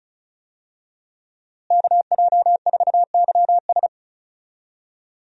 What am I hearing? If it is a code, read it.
Morse code, heard as a keyed tone, about 36 wpm, 700 Hz. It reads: KJ4YS
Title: USA Call Signs - 5 Characters - Repeated 3x - 35wpm